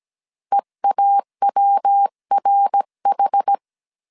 Morse code, heard as a keyed tone, about 17 wpm, 800 Hz. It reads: EAWRH
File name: CW / DET / 1 3 1500C S A 16wpm 2141